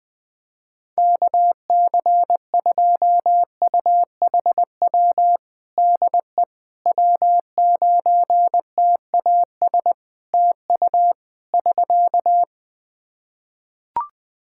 Morse code, heard as a keyed tone, about 20 wpm, 700 Hz. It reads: KC2UHW DE W9TAS TU <SK>
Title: Practice QSO - Exchange 4 - Ending - 20wpm